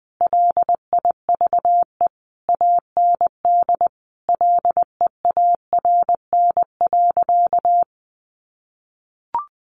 Morse code, heard as a keyed tone, about 20 wpm, 700 Hz. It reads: LIVE AND LEARN.
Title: Sentences from Top 800 Words 20wpm